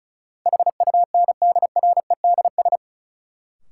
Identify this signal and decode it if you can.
Morse code, heard as a keyed tone, about 34 wpm, 700 Hz. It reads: HUNDREDS
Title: Sets of 4 Words - New Word Drill - 35wpm